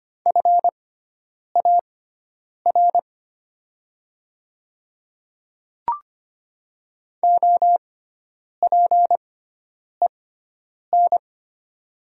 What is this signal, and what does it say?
Morse code, heard as a keyed tone, about 25 wpm, spaced as for 9 wpm, 700 Hz. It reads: FAR OPEN
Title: 6x Character Spacing - Top 300 Words - 25wpm